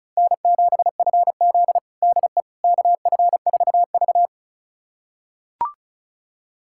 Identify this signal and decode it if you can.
Morse code, heard as a keyed tone, about 36 wpm, 700 Hz. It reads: N7FZ DE KF4V
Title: USA Callsign DE Callsign - 4 Characters - 35wpm